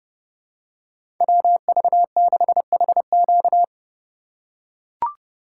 Morse code, heard as a keyed tone, about 32 wpm, 700 Hz. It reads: WV6HQ
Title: USA Call Signs - 5 Characters - 3x Character Spacing - 30wpm